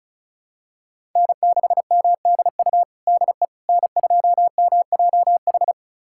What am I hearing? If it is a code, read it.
Morse code, heard as a keyed tone, about 36 wpm, 700 Hz. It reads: N6MDU DE N2MJH
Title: USA Callsign DE Callsign - 5 Characters - 35wpm